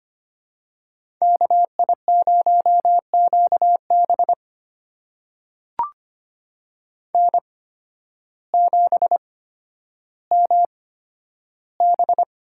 Morse code, heard as a keyed tone, about 25 wpm, spaced as for 12 wpm, 700 Hz. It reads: KI0QB N7MB
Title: USA Call Signs - 5 Characters - 8x Character Spacing - 25wpm